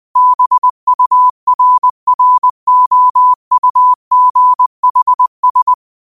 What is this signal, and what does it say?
Morse code, heard as a keyed tone, about 20 wpm, 1000 Hz. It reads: BURROUGHS
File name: qotdfiles / 20wpm